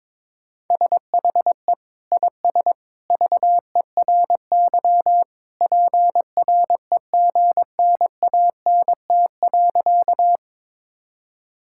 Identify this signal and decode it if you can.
Morse code, heard as a keyed tone, about 22 wpm, 700 Hz. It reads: SHE IS VERY PREGNANT.
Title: Sentences from Top 1000 Words - Review Part 6 - 22wpm